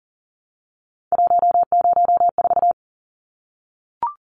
Morse code, heard as a keyed tone, about 40 wpm, 700 Hz. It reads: 104
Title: Numbers - 3-Digit 40wpm